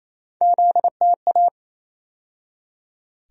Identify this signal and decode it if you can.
Morse code, heard as a keyed tone, about 28 wpm, 700 Hz. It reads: ZTA
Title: USA Call Sign Postfixes - 3 Letters 28wpm